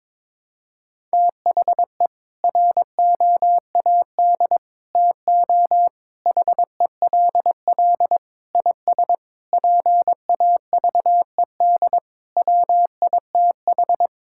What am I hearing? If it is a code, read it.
Morse code, heard as a keyed tone, about 22 wpm, 700 Hz. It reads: THE ROAD TO HELL IS PAVED WITH
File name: practice / 22wpm / Sentences with 8-Word Phrases - 22wpm